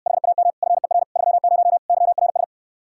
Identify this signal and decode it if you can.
Morse code, heard as a keyed tone, about 68 wpm, 700 Hz. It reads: HAM FER 30 YRS